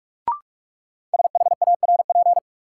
Morse code, heard as a keyed tone, about 44 wpm, 700 Hz. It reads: SHARP
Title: Sentences from Top 1600 Words - New Word Drill - 45wpm